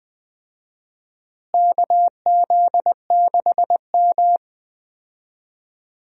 Morse code, heard as a keyed tone, about 21 wpm, 700 Hz. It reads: KZ6M
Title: USA Call Signs - 4 Characters - Repeated 3x - 20wpm